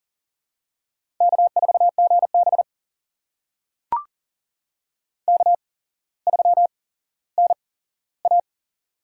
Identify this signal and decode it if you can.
Morse code, heard as a keyed tone, about 40 wpm, spaced as for 17 wpm, 700 Hz. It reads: K4GB K3NA
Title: USA Call Signs - 8x Character Spacing - 40wpm